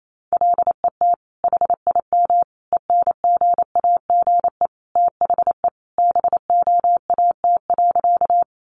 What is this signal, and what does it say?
Morse code, heard as a keyed tone, about 28 wpm, 700 Hz. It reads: LET HIM ENGAGE THE BOAT.
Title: Sentences from Top 1500 Words 28wpm